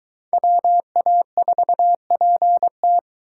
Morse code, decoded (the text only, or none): WA4PT